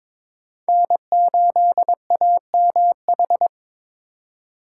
Morse code, heard as a keyed tone, about 22 wpm, 700 Hz. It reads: N8AMH
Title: USA Call Signs - 5 Characters - 2x Character Spacing - 22wpm